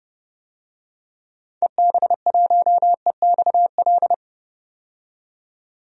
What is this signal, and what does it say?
Morse code, heard as a keyed tone, about 30 wpm, 700 Hz. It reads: EB1EXL
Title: DXCC-100 Call Signs - 6 Characters - 30wpm